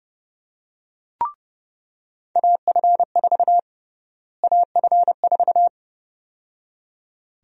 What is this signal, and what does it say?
Morse code, heard as a keyed tone, about 30 wpm, 700 Hz. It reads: AF4 AF4
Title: USA Call Sign Prefixes - Repeated 2x - 30wpm